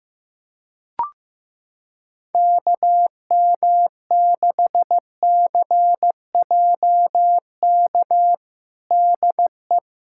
Morse code, heard as a keyed tone, about 15 wpm, 700 Hz. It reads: KM6CJK DE
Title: USA Callsign DE Callsign - 6 Characters - 15wpm